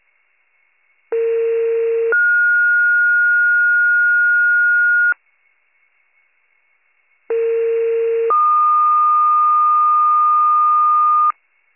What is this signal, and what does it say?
Two-tone page, heard as a one-second tone, then a three-second tone, about 460 then 1430 Hz, about 460 then 1220 Hz. It is heard from 1.1 to 11.3 s.